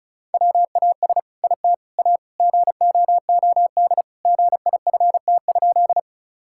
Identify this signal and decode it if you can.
Morse code, heard as a keyed tone, about 36 wpm, 700 Hz. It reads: WAS IT A GOOD GIFT?
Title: Sentences from Top 1000 Words - Review Part 6 - 35wpm